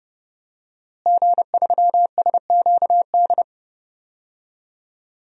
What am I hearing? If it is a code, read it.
Morse code, heard as a keyed tone, about 30 wpm, 700 Hz. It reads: G3SQD
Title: DXCC-100 Call Signs - 5 Characters - Repeated 3x - 30wpm